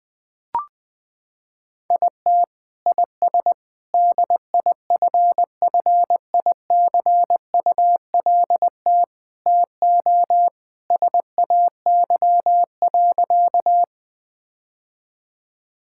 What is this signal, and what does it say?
Morse code, heard as a keyed tone, about 20 wpm, 700 Hz. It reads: IT IS DIFFICULT TO SAY.
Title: Sentences with 4-Word Phrases - 20wpm